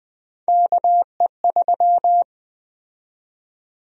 Morse code, heard as a keyed tone, about 21 wpm, 700 Hz. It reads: KE3
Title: USA Call Sign Prefixes - 20wpm